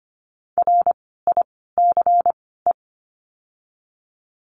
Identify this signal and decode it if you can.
Morse code, heard as a keyed tone, about 25 wpm, 700 Hz. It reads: R I C E
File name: course / 25:10wpm / 094 Lesson 16f - Letter c - Words 25wpm